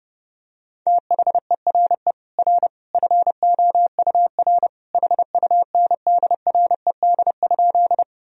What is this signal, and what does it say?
Morse code, heard as a keyed tone, about 31 wpm, 700 Hz. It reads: THERE R FOUR HUNDRED?
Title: Sentences from Top 400 Words and Abbreviations 30wpm